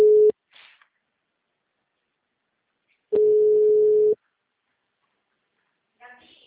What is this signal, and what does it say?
Ringback tone (Europe, 425 Hz), with speech.